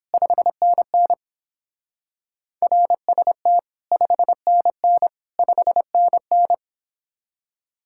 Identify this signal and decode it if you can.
Morse code, heard as a keyed tone, about 30 wpm, 700 Hz. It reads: 5NN RST 5NN 5NN